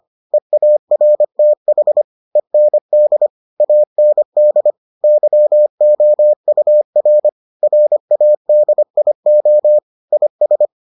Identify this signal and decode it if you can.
Morse code, heard as a keyed tone, about 25 wpm, 600 Hz. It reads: EARTH END AND YOUR RADIO IS